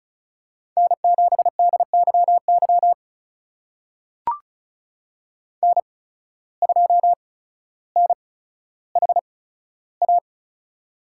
Morse code, heard as a keyed tone, about 36 wpm, spaced as for 15 wpm, 700 Hz. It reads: N7DYY N2NHA
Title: USA Call Signs - 5 Characters - 8x Character Spacing - 35wpm